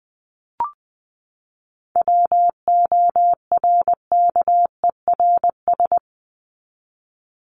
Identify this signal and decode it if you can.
Morse code, heard as a keyed tone, about 20 wpm, 700 Hz. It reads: WORKERS
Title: ES:S Suffix - Top 100 Words with Base Words 20wpm